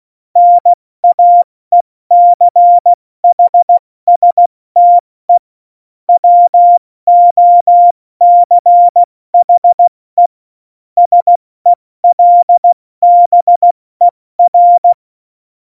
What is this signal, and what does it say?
Morse code, heard as a keyed tone, about 16 wpm, spaced as for 14 wpm, 700 Hz. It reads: NAECHSTE WOCHE SELBER